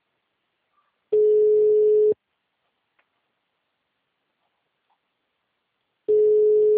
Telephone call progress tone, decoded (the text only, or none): ringback tone